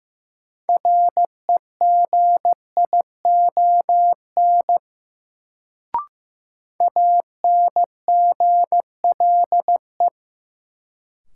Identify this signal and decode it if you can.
Morse code, heard as a keyed tone, about 15 wpm, 700 Hz. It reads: REGION ANGLE
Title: Sentences from Top 1500 Words - New Word Drill - 15wpm